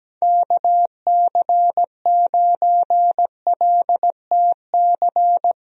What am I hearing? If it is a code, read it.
Morse code, heard as a keyed tone, about 17 wpm, 700 Hz. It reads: KC9LTC
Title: USA Call Signs - 6 Characters - Repeated 2x - 17wpm